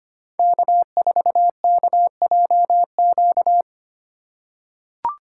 Morse code, heard as a keyed tone, about 25 wpm, 700 Hz. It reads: K4KJQ
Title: USA Call Signs - 5 Characters - 4x Character Spacing - 25wpm